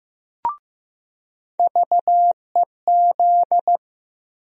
Morse code, heard as a keyed tone, about 15 wpm, 700 Hz. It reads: VEZ
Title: USA Call Sign Postfixes - 3 Letters 15wpm